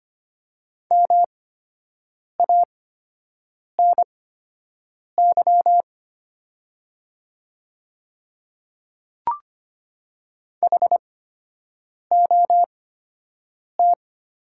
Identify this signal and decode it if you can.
Morse code, heard as a keyed tone, about 25 wpm, spaced as for 7 wpm, 700 Hz. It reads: MANY HOT
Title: 8x Character Spacing - Top 100 Words - 25wpm